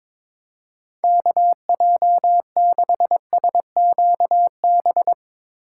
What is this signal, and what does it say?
Morse code, heard as a keyed tone, about 22 wpm, 700 Hz. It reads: KJ6SQB
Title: USA Call Signs - 6 Characters - 3x Character Spacing - 22wpm